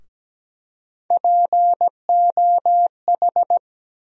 Morse code, heard as a keyed tone, about 17 wpm, 700 Hz. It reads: POH
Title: USA Call Sign Postfixes - 3 Letters 17wpm